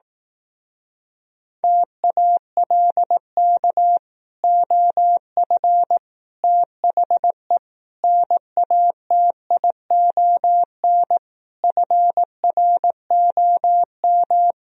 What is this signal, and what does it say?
Morse code, heard as a keyed tone, about 18 wpm, 700 Hz. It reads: TALK OF THE NATION FROM
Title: Speed-Racing - Sets of 5 Words 15wpm